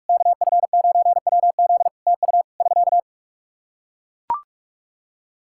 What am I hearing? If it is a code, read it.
Morse code, heard as a keyed tone, about 41 wpm, 700 Hz. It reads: KF9WZ TU <SK>
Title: Practice QSO - Exchange 4 - Ending - 45wpm